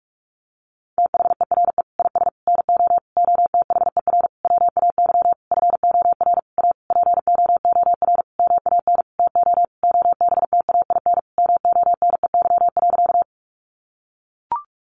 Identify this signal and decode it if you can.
Morse code, heard as a keyed tone, about 44 wpm, 700 Hz. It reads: THERE IS NO OTHER WAY FOR A POOR MAN TO OBTAIN MONEY.
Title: Sentences from Top 1300 Words 45wpm